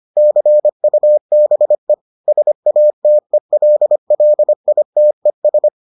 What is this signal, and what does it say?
Morse code, heard as a keyed tone, about 25 wpm, 600 Hz. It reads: CUBE SATELLITES